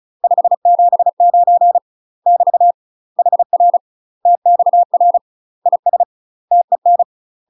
Morse code, heard as a keyed tone, about 36 wpm, 700 Hz. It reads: 579 <BT> HR TXR IS TEN
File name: UK MASTER QSOs TRAINING / 35WPM / UK MASTER QSOs 0026